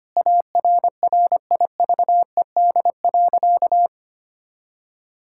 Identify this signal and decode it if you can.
Morse code, heard as a keyed tone, about 25 wpm, 700 Hz. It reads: ARRIVED.
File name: practice / 25wpm / Sentences with 2-Word Phrases - 25wpm